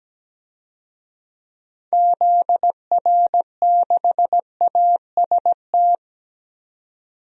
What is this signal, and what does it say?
Morse code, heard as a keyed tone, about 17 wpm, 700 Hz. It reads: ZR6AST
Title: DXCC-100 Call Signs - 6 Characters - 17wpm